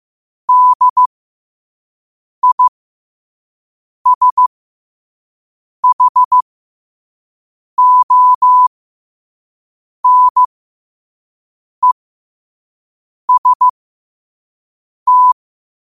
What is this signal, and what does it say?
Morse code, heard as a keyed tone, about 15 wpm, spaced as for 5 wpm, 1000 Hz. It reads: DISHONEST
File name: qotdfiles / 5wpm